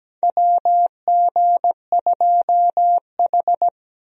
Morse code, heard as a keyed tone, about 17 wpm, 700 Hz. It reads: WG2H
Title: USA Call Signs - 4 Characters - Repeated 2x - 17wpm